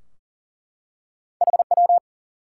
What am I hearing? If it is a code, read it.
Morse code, heard as a keyed tone, about 40 wpm, 700 Hz. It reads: HW
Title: Top 90 QSO Elements 40wpm